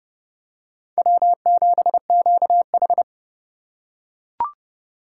Morse code, heard as a keyed tone, about 32 wpm, 700 Hz. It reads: W7QH
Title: USA Call Signs - 3x Character Spacing - 30wpm